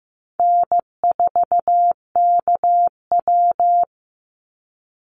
Morse code, heard as a keyed tone, about 15 wpm, 700 Hz. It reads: N4KW
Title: Easy Call Signs - 15wpm